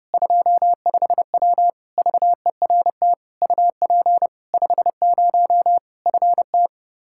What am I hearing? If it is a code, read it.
Morse code, heard as a keyed tone, about 32 wpm, 700 Hz. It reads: 25W VERT UP 50 FT
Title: Practice QSO - Exchange 2 - Info - 30wpm